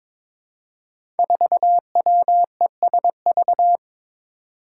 Morse code, heard as a keyed tone, about 22 wpm, 700 Hz. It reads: 4WESV
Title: Five Challenging Letter-Number Combinations 22wpm